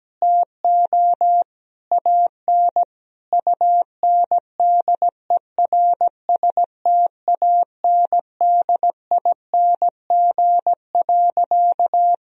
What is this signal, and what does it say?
Morse code, heard as a keyed tone, about 17 wpm, 700 Hz. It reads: TO AN UNDERSTANDING.
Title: Sentences from Top 1000 Words - Review Part 3 - 17wpm